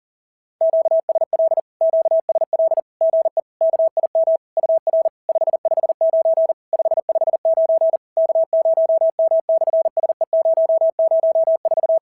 Morse code, heard as a keyed tone, about 40 wpm, 690 Hz. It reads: QSL QSL GE KIM UR 559 559 K0M/SE004
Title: SOTA QSO Practice 40wpm